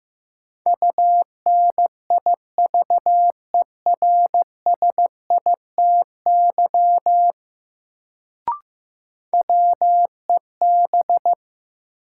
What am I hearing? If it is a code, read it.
Morse code, heard as a keyed tone, about 15 wpm, 700 Hz. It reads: UNIVERSITY WEB
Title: Sentences from Top 1000 Words - Review Part 4 - New Word Drill - 15wpm